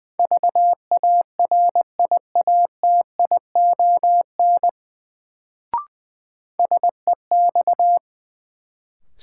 Morse code, heard as a keyed tone, about 21 wpm, 700 Hz. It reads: VARIATION SEX
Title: Sentences from Top 1600 Words - New Word Drill - 20wpm